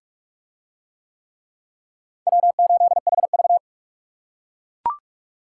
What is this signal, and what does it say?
Morse code, heard as a keyed tone, about 44 wpm, 700 Hz. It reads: W8HV